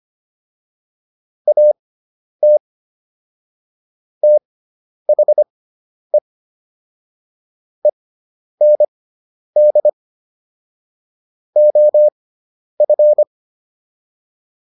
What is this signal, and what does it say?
Morse code, heard as a keyed tone, about 25 wpm, spaced as for 10 wpm, 600 Hz. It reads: AT THE END OF